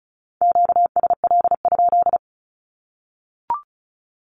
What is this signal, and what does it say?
Morse code, heard as a keyed tone, about 34 wpm, 700 Hz. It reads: QSL?
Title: Q-Codes 35wpm